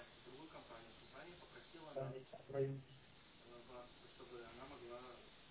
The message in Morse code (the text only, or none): none